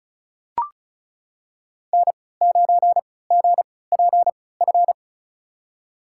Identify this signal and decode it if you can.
Morse code, heard as a keyed tone, about 36 wpm, spaced as for 20 wpm, 700 Hz. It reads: N9GPF
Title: USA Call Signs - 5 Characters - 3x Character Spacing - 35wpm